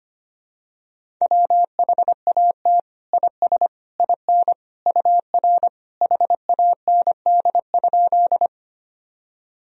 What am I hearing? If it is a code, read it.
Morse code, heard as a keyed tone, about 25 wpm, 700 Hz. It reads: WHAT IS IN UR HAND?
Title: Sentences from Top 200 Words and Abbreviations 25wpm